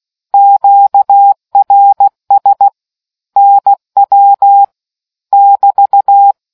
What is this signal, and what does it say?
Morse code, heard as a keyed tone, about 16 wpm, 800 Hz. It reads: QRS NW <BT>